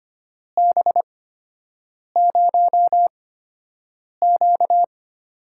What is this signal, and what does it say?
Morse code, heard as a keyed tone, about 25 wpm, spaced as for 7 wpm, 700 Hz. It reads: B0Q
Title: USA Call Signs - 6 Characters - 8x Character Spacing - 25wpm